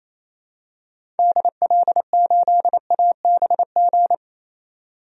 Morse code, heard as a keyed tone, about 28 wpm, 700 Hz. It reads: DL8ABG
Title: DXCC-100 Call Signs - 6 Characters - 28wpm